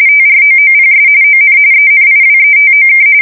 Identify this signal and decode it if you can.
RTTY, about 45.45 baud. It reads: TROOPS ON ENIWETOK
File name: fsk170 / battleshipnj